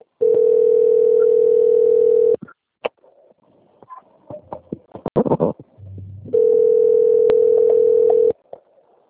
Ringback tone (North America), 440 + 480 Hz.